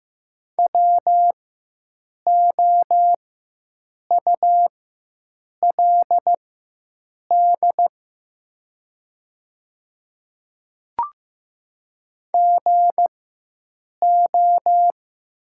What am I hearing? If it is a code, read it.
Morse code, heard as a keyed tone, about 15 wpm, spaced as for 7 wpm, 700 Hz. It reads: WOULD GO
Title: 4x Character Spacing - Top 100 Words - 15wpm